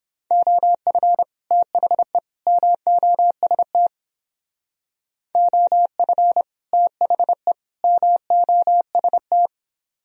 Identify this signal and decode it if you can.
Morse code, heard as a keyed tone, about 31 wpm, 700 Hz. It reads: OF THE MOST OF THE MOST